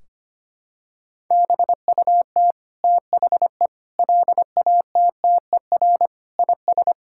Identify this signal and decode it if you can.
Morse code, heard as a keyed tone, about 25 wpm, 700 Hz. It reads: BUT THE LATTER IS
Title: Sentences from Top 1600 Words 25wpm